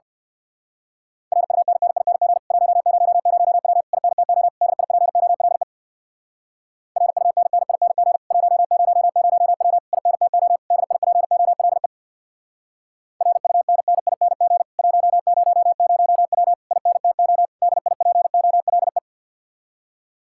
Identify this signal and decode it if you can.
Morse code, heard as a keyed tone, about 58 wpm, 700 Hz. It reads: RUNNING 100W INTO DIPOLE RUNNING 100W INTO DIPOLE RUNNING 100W INTO DIPOLE
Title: Speed-Racing Common QSO Phrases - 50wpm